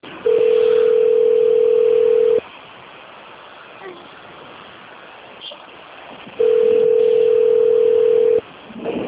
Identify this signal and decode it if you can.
Ringback tone (North America), 440 + 480 Hz.